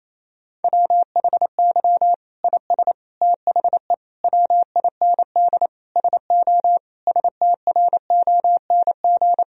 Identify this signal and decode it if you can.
Morse code, heard as a keyed tone, about 28 wpm, 700 Hz. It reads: WHY IS THE WIND SO STRONG
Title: Sentences from Top 1000 Words - Review Part 2 - 28wpm